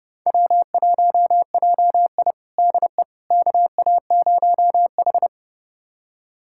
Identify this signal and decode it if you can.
Morse code, heard as a keyed tone, about 30 wpm, 700 Hz. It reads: W1JI DE KA0H
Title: USA Callsign DE Callsign - 4 Characters - 30wpm